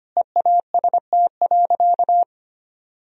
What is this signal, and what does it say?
Morse code, heard as a keyed tone, about 25 wpm, 700 Hz. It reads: EAST.
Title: Sentences from Top 1000 Words - Review Part 3 - 25wpm